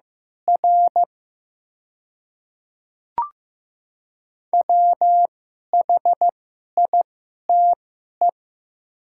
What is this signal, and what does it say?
Morse code, heard as a keyed tone, about 15 wpm, spaced as for 11 wpm, 700 Hz. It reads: R WHITE
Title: 2x Character Spacing - Top 300 Words - 15wpm